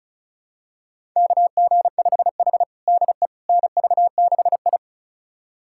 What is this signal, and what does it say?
Morse code, heard as a keyed tone, about 36 wpm, 700 Hz. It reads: KG5H DE NV6I
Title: USA Callsign DE Callsign - 4 Characters - 35wpm